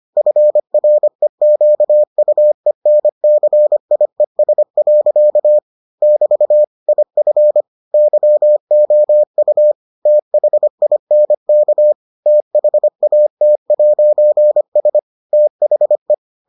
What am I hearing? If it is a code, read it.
Morse code, heard as a keyed tone, about 25 wpm, 600 Hz. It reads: FREQUENCIES. <BT> IF YOU THINK THAT'S THE